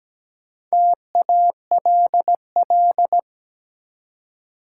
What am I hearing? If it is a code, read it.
Morse code, heard as a keyed tone, about 17 wpm, 700 Hz. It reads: TALL
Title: Four Letter Words - 17wpm